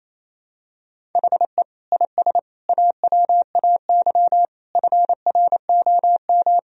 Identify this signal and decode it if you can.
Morse code, heard as a keyed tone, about 28 wpm, 700 Hz. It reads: HE IS AWAY FROM
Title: Sentences from Top 1000 Words - Review Part 1 - 28wpm